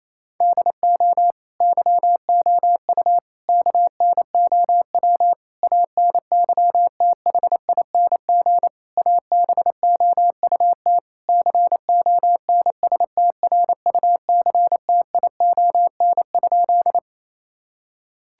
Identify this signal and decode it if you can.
Morse code, heard as a keyed tone, about 28 wpm, 700 Hz. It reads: DO YOU KNOW ANYTHING ABOUT CONSTRUCTION?
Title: Sentences from Top 1600 Words 28wpm